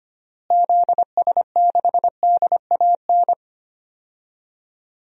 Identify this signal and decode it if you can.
Morse code, heard as a keyed tone, about 25 wpm, 700 Hz. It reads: ZS6DAN
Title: DXCC-100 Call Signs - 6 Characters - Repeated 3x - 25wpm